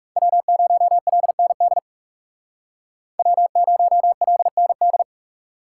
Morse code, heard as a keyed tone, about 35 wpm, 700 Hz. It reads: W0LND W0LND
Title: Speed Racing - USA 1x3 Call Signs 30wpm